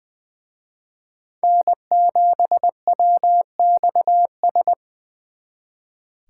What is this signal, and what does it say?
Morse code, heard as a keyed tone, about 24 wpm, 700 Hz. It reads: N7WXS
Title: Speed Racing - USA Call Signs 20wpm